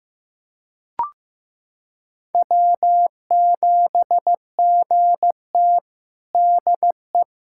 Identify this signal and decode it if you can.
Morse code, heard as a keyed tone, about 15 wpm, 700 Hz. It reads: W7GT DE
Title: USA Callsign DE Callsign - 4 Characters - 15wpm